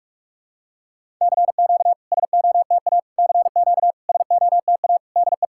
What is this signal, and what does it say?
Morse code, heard as a keyed tone, about 44 wpm, 690 Hz. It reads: CQ SOTA CQ SOTA DE
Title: SOTA QSO Practice 45wpm